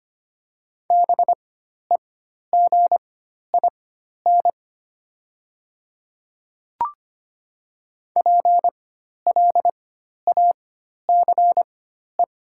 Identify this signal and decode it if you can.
Morse code, heard as a keyed tone, about 25 wpm, spaced as for 12 wpm, 690 Hz. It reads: BEGIN PLACE
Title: 4x Character Spacing - Top 500 Words - 25wpm